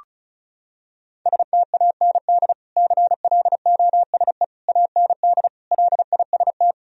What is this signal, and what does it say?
Morse code, heard as a keyed tone, about 36 wpm, 700 Hz. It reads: STAND CLOSE AND LIST
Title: Sentences from Top 1000 Words - Review Part 2 - 35wpm